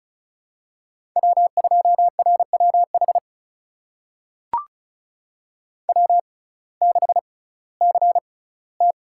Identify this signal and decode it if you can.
Morse code, heard as a keyed tone, about 36 wpm, spaced as for 19 wpm, 700 Hz. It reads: W2RWH W6CT
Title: USA Call Signs - 5 Characters - 6x Character Spacing - 35wpm